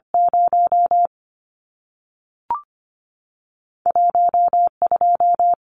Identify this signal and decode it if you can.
Morse code, heard as a keyed tone, about 25 wpm, 700 Hz. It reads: 0 12
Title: Numbers - 2-Digit 25wpm